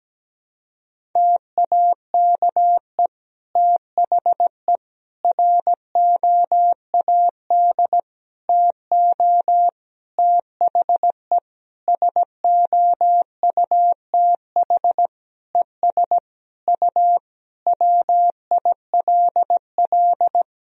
Morse code, heard as a keyed tone, about 17 wpm, 700 Hz. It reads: TAKE THE ROAD TO THE SOUTH ES U WILL
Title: Sentences from Top 400 Words and Abbreviations 17wpm